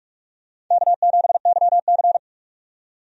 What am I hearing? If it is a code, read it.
Morse code, heard as a keyed tone, about 47 wpm, 700 Hz. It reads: K7YC
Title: USA Call Signs - 4 Characters - 45wpm